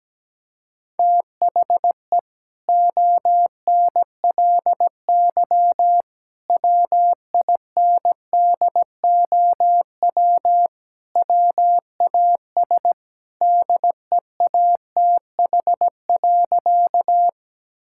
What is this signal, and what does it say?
Morse code, heard as a keyed tone, about 17 wpm, 700 Hz. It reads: THE ONLY WINDOW WAS DEATH.
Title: Sentences from Top 1600 Words 17wpm